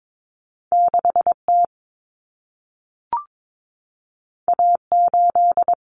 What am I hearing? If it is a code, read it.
Morse code, heard as a keyed tone, about 22 wpm, 700 Hz. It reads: 6T A8
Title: Common Cut Numbers - 2-Digit 22wpm